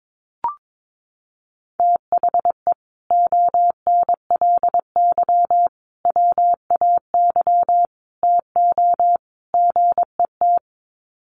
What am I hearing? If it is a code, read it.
Morse code, heard as a keyed tone, about 22 wpm, 700 Hz. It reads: THE ONLY WAY TO GET
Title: Sets of 5 Words Encore2 22wpm